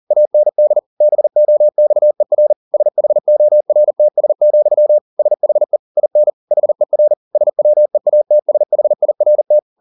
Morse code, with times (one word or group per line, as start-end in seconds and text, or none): 0.1-0.8 AND
1.0-2.5 BOXER
2.7-5.0 SHORTS,
5.2-5.8 SHE
6.0-6.3 IN
6.5-7.1 HER
7.4-9.6 SWEATSHIRT